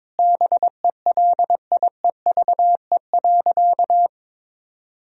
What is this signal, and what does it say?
Morse code, heard as a keyed tone, about 22 wpm, 700 Hz. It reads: BELIEVE.
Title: Sentences from Top 1000 Words - Review Part 4 - 22wpm